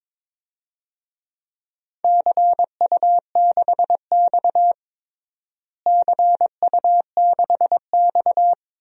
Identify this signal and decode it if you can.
Morse code, heard as a keyed tone, about 22 wpm, 700 Hz. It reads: CU6X CU6X
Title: DXCC-100 Call Signs - 4 Characters - Repeated 2x - 22wpm